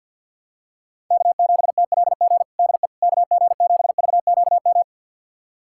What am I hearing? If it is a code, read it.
Morse code, heard as a keyed tone, about 45 wpm, 700 Hz. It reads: K7TLG DE KG7VQM
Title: USA Callsign DE Callsign 50wpm